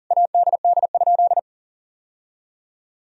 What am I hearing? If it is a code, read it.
Morse code, heard as a keyed tone, about 40 wpm, 700 Hz. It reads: ADD?
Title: Sentences from Top 1000 Words - Review Part 1 - 40wpm